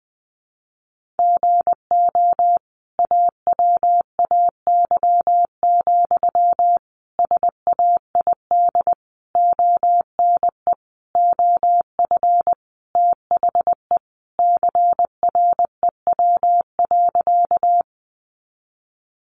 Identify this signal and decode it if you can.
Morse code, heard as a keyed tone, about 20 wpm, 700 Hz. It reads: GO AWAY, SAID ONE OF THE CREW.
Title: Sentences from Top 1500 Words 20wpm